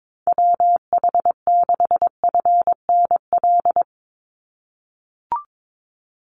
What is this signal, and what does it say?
Morse code, heard as a keyed tone, about 22 wpm, 700 Hz. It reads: WH6FNL
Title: USA Call Signs 22wpm